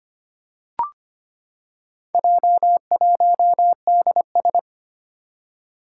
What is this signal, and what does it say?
Morse code, heard as a keyed tone, about 25 wpm, 700 Hz. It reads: J1DS
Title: Four Challenging Letter-Number Combinations 25wpm